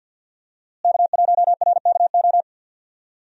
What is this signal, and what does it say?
Morse code, heard as a keyed tone, about 52 wpm, 700 Hz. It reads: K1RKO
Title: USA Call Signs - 5 Characters - Repeated 2x - 50wpm